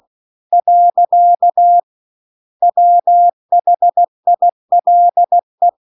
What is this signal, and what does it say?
Morse code, heard as a keyed tone, about 16 wpm, 700 Hz. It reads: . WHILE